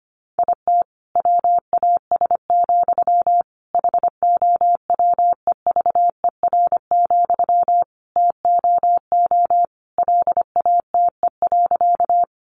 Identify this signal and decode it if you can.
Morse code, heard as a keyed tone, about 25 wpm, 700 Hz. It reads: IT WAS, HOWEVER, TOO LATE.
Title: Sentences from Top 600 Words 25wpm